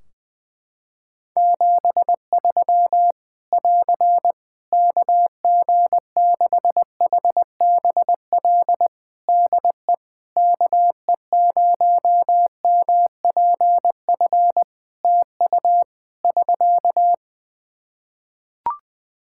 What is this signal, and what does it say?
Morse code, heard as a keyed tone, about 20 wpm, 700 Hz. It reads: 73 <AR> KG6HBL DE KE0MPF TU <SK>
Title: Practice QSO - Exchange 4 - Ending - 20wpm